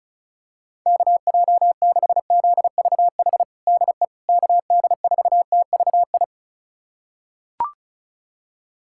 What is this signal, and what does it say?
Morse code, heard as a keyed tone, about 36 wpm, 700 Hz. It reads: KJ6ZVH DE KD4TVI
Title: USA Callsign DE Callsign 35wpm